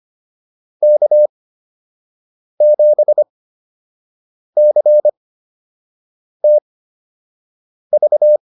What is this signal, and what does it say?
Morse code, heard as a keyed tone, about 25 wpm, spaced as for 6 wpm, 600 Hz. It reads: K7CTV
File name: QSO 204 06